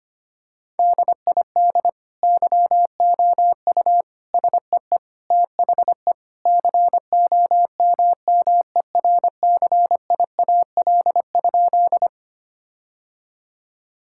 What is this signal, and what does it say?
Morse code, heard as a keyed tone, about 25 wpm, 700 Hz. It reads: DID YOU SEE THE COMMERCIAL?
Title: Sentences from Top 1000 Words - Review Part 5 - 25wpm